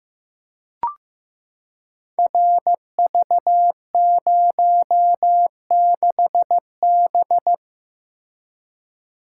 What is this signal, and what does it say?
Morse code, heard as a keyed tone, about 15 wpm, 700 Hz. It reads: RV06B
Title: Code Groups - 5 Letter:Numbers - 15wpm